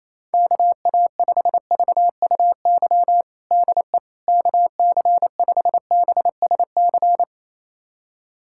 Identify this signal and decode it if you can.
Morse code, heard as a keyed tone, about 28 wpm, 700 Hz. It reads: KA5VUY DE KC5BSC
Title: USA Callsign DE Callsign - 6 Characters - 28wpm